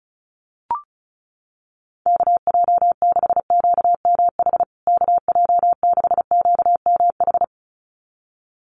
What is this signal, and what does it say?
Morse code, heard as a keyed tone, about 34 wpm, 700 Hz. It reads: KJ6QMH KJ6QMH
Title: USA Call Signs Repeated 2x 35wpm